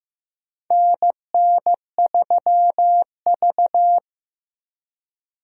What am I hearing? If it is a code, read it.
Morse code, heard as a keyed tone, about 15 wpm, 700 Hz. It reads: NN3V
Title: USA Call Signs - 4 Characters - Repeated 2x - 15wpm